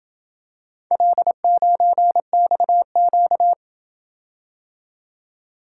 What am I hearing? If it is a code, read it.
Morse code, heard as a keyed tone, about 26 wpm, 700 Hz. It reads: L9XQ